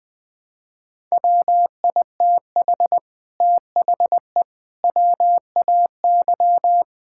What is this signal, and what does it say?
Morse code, heard as a keyed tone, about 20 wpm, 700 Hz. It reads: WITH THE WAY
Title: Sentences with 8-Word Phrases - 20wpm